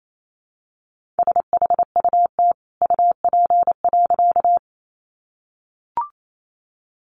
Morse code, heard as a keyed tone, about 28 wpm, 700 Hz. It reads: SHUT UP.
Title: A Sound Of Thunder 28wpm